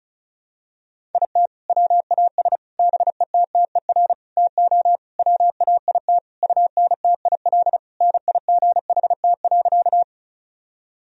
Morse code, heard as a keyed tone, about 36 wpm, 700 Hz. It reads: IT WAS BETTER TO WAIT UNTIL NIGHT.
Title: Sentences from Top 1000 Words - Review Part 1 - 35wpm